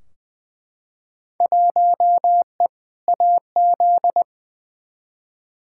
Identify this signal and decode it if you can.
Morse code, heard as a keyed tone, about 20 wpm, 700 Hz. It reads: 1E AZ
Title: Field Day Exchanges - 20wpm